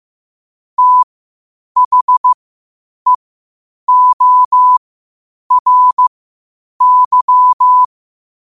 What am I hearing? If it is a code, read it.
Morse code, heard as a keyed tone, about 15 wpm, spaced as for 8 wpm, 1000 Hz. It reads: THEORY
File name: qotdfiles / 8wpm